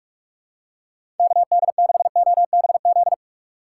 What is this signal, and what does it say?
Morse code, heard as a keyed tone, about 46 wpm, 700 Hz. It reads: KD6OBZ
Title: USA Call Signs - 3x Character Spacing - 45wpm